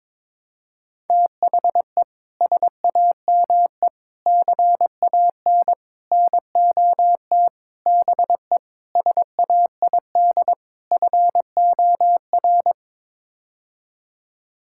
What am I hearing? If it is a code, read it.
Morse code, heard as a keyed tone, about 22 wpm, 700 Hz. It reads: THE SAME CAN NOT BE SAID FOR
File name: practice / 22wpm / Sets of 7 Words - 22wpm